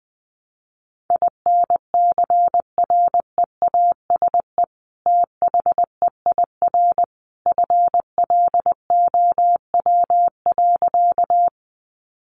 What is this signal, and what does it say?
Morse code, heard as a keyed tone, about 20 wpm, 700 Hz. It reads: INCREASE THEIR FLOW.
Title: Sentences from Top 1200 Words 20wpm